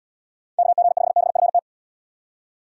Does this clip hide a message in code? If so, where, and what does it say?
Yes: Morse code from 0.6 to 1.6 s, heard as a keyed tone, about 78 wpm, 700 Hz. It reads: KG5BFT